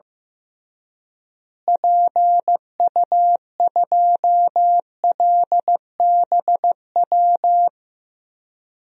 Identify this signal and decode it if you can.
Morse code, heard as a keyed tone, about 15 wpm, 700 Hz. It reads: PU2LBW
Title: DXCC-100 Call Signs - 6 Characters - Repeated 3x - 15wpm